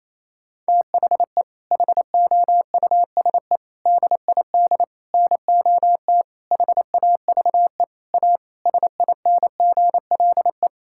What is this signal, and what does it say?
Morse code, heard as a keyed tone, about 28 wpm, 700 Hz. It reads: THE HOUSE DID NOT HAVE A SINGLE
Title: Sentences from Top 1600 Words 28wpm